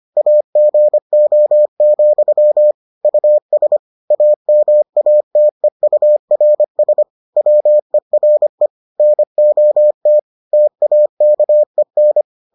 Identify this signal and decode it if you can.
Morse code, heard as a keyed tone, about 25 wpm, 600 Hz. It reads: AGO, US AMATEURS WERE NOT TAKEN